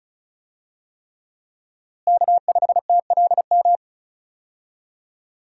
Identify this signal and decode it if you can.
Morse code, heard as a keyed tone, about 34 wpm, 700 Hz. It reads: K5TLM
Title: USA Call Signs - 5 Characters - Repeated 3x - 35wpm